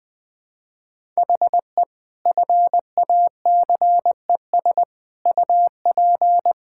Morse code, heard as a keyed tone, about 20 wpm, 700 Hz. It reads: HE FACES UP
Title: Sentences with 6-Word Phrases - 20wpm